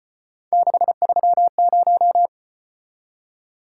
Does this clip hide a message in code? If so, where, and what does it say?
Yes: Morse code from 0.5 to 2.3 s, heard as a keyed tone, about 33 wpm, 700 Hz. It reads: 630